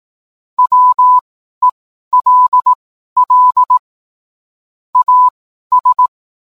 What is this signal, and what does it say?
Morse code, heard as a keyed tone, about 18 wpm, spaced as for 12 wpm, 1000 Hz. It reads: WELL AS